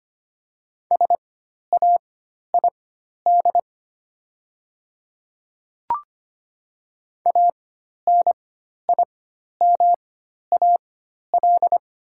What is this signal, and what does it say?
Morse code, heard as a keyed tone, about 25 wpm, spaced as for 12 wpm, 700 Hz. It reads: SAID ANIMAL
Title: 4x Character Spacing - Top 200 Words - 25wpm